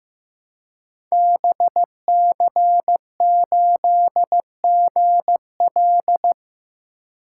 Morse code, heard as a keyed tone, about 15 wpm, 700 Hz. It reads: BC8GL
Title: Code Groups - 5 Letter:Numbers - 15wpm